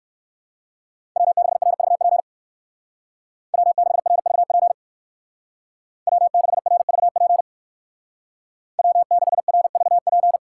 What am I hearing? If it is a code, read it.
Morse code, heard as a keyed tone, about 60 wpm, 700 Hz. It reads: W6RVP W6RVP W6RVP W6RVP